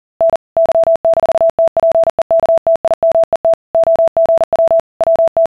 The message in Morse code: N Y-TPEKTIMET OGW WT